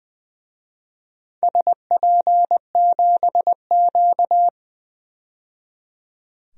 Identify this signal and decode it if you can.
Morse code, heard as a keyed tone, about 20 wpm, 700 Hz. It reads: SP7Q
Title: DXCC-100 Call Signs - 4 Characters - 20wpm